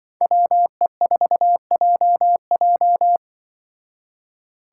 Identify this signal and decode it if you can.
Morse code, heard as a keyed tone, about 24 wpm, 700 Hz. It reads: WE4JJ
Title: Speed Racing - USA 2x2 Call Signs 20wpm